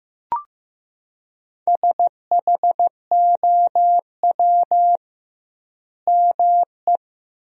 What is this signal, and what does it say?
Morse code, heard as a keyed tone, about 15 wpm, 700 Hz. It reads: SHOW ME
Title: 2x Word-Spacing - Sentences from Top 200 Words - 15wpm